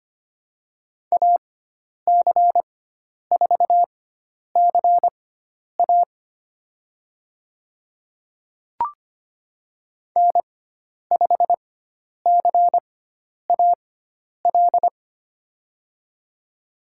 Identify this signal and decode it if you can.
Morse code, heard as a keyed tone, about 25 wpm, spaced as for 10 wpm, 700 Hz. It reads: AC4CA N5CAL
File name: Lesson 16g - Letter c - Call Signs 25wpm